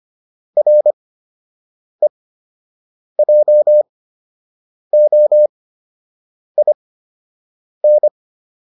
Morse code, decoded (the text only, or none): REJOIN